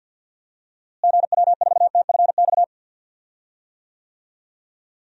Morse code, heard as a keyed tone, about 52 wpm, 700 Hz. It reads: GW4TFX
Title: DXCC-100 Call Signs - 6 Characters - Repeated 2x - 50wpm